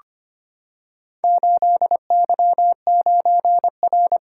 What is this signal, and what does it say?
Morse code, heard as a keyed tone, about 25 wpm, 700 Hz. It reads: 8Y9R
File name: practice / 25wpm / Four Letter-Number Combinations - 25wpm